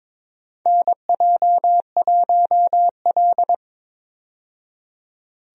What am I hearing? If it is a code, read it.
Morse code, heard as a keyed tone, about 22 wpm, 700 Hz. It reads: NJ1L
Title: USA Call Signs - 4 Characters - 22wpm